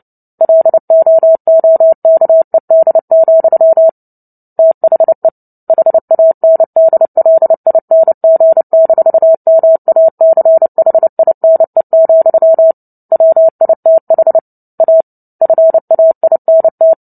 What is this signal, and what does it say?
Morse code, heard as a keyed tone, about 30 wpm, 650 Hz. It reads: LOOKED, THE HANDLING-MACHINE, WITH A FAINT
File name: Book Two / 03-Chapter 3 - 29 WPM